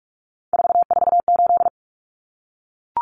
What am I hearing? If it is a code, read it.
Morse code, heard as a keyed tone, about 44 wpm, 700 Hz. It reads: 448
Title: Common Cut Numbers - 3-Digit 45wpm